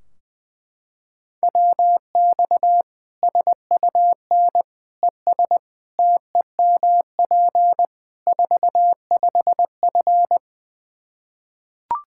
Morse code, heard as a keyed tone, about 21 wpm, 700 Hz. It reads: WX SUN ES TEMP 45F
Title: Practice QSO - Exchange 2 - Info - 20wpm